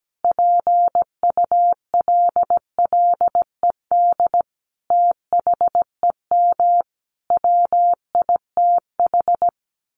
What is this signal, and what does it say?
Morse code, heard as a keyed tone, about 17 wpm, 700 Hz. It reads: PULLED THEM WITH
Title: A Sound Of Thunder 17wpm